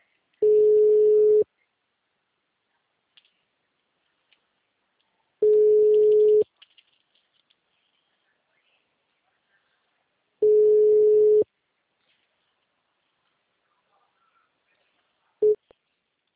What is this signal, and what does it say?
Ringback tone (Europe, 425 Hz).